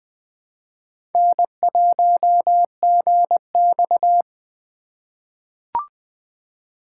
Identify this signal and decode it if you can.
Morse code, heard as a keyed tone, about 20 wpm, 700 Hz. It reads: N1GX
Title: USA Call Signs - 3x Character Spacing - 20wpm